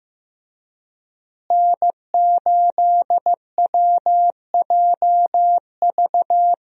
Morse code, heard as a keyed tone, about 15 wpm, 700 Hz. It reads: N8WJV
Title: USA Call Signs - 5 Characters - Repeated 3x - 15wpm